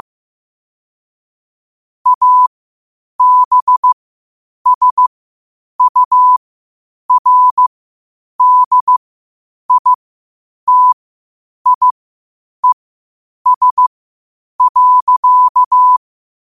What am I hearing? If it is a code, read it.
Morse code, heard as a keyed tone, about 15 wpm, spaced as for 8 wpm, 1000 Hz. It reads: ABSURDITIES.
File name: qotdfiles / 8wpm